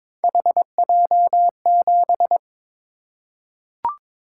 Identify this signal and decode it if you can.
Morse code, heard as a keyed tone, about 22 wpm, 700 Hz. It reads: HJ7
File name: practice / 22wpm / Three Letter-Number Combinations - 22wpm